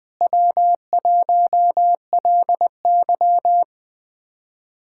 Morse code, heard as a keyed tone, about 20 wpm, 700 Hz. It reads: W1LY
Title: USA Callsign DE Callsign - 4 Characters - 20wpm